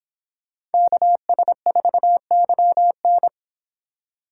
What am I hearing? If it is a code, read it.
Morse code, heard as a keyed tone, about 28 wpm, 700 Hz. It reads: KS4YN